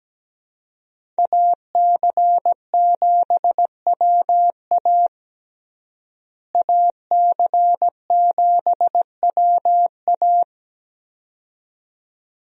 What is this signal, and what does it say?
Morse code, heard as a keyed tone, about 17 wpm, 700 Hz. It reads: AC7WA AC7WA
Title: USA Call Signs - 5 Characters - Repeated 2x - 17wpm